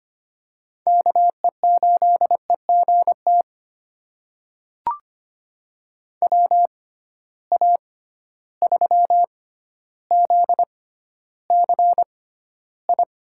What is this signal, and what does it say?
Morse code, heard as a keyed tone, about 25 wpm, spaced as for 13 wpm, 700 Hz. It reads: KE8EGT WA3ZCI
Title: USA Call Signs - 6 Characters - 6x Character Spacing - 25wpm